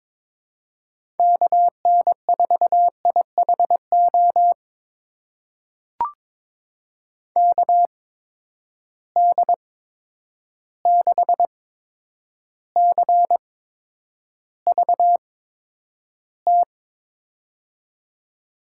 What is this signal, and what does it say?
Morse code, heard as a keyed tone, about 22 wpm, spaced as for 9 wpm, 700 Hz. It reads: KN4IHO KD6CVT
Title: USA Call Signs - 6 Characters - 8x Character Spacing - 22wpm